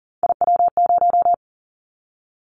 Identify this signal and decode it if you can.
Morse code, heard as a keyed tone, about 42 wpm, 700 Hz. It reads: IW0
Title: Callsign Prefixes for Top 50 DXCC - 40wpm